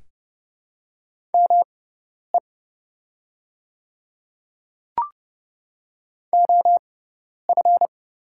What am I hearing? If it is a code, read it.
Morse code, heard as a keyed tone, about 30 wpm, spaced as for 10 wpm, 700 Hz. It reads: ME OF
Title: 6x Character Spacing - Top 300 Words - 30wpm